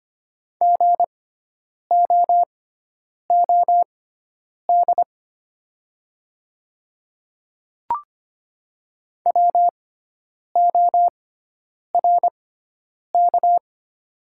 Morse code, heard as a keyed tone, about 25 wpm, spaced as for 9 wpm, 700 Hz. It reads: GOOD WORK
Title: 6x Character Spacing - Top 200 Words - 25wpm